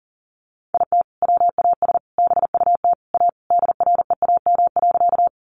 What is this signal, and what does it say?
Morse code, heard as a keyed tone, about 40 wpm, 700 Hz. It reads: IT WAS BUT A DREAM.
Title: Sentences from Top 1500 Words 40wpm